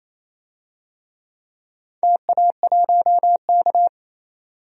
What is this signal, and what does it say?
Morse code, heard as a keyed tone, about 27 wpm, 690 Hz. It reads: TA1K
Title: DXCC-100 Call Signs - 4 Characters - 28wpm